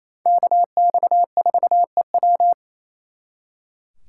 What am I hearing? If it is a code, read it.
Morse code, heard as a keyed tone, about 36 wpm, 700 Hz. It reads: KX4EW